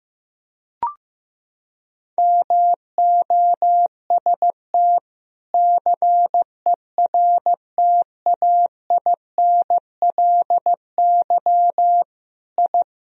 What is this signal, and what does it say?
Morse code, heard as a keyed tone, about 15 wpm, 700 Hz. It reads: MOST CERTAINLY I
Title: Sentences from Top 1000 Words - Review Part 1 - 15wpm